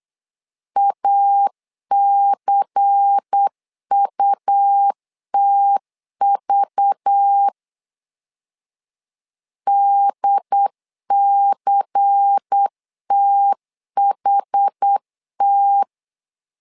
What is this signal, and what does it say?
Morse code, heard as a keyed tone, about 8 wpm, 800 Hz. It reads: ACUTV DCTHT
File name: LEZIONI / 5 LBVC / 5 LBVC 8wpm 2935